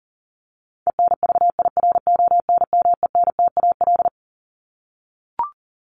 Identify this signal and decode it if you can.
Morse code, heard as a keyed tone, about 38 wpm, 700 Hz. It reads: ENVIRONMENTAL
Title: Top 1001-1250 Words 40wpm